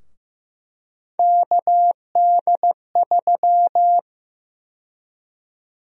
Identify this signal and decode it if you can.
Morse code, heard as a keyed tone, about 15 wpm, 700 Hz. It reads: KD3
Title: USA Call Sign Prefixes - Repeated 2x - 15wpm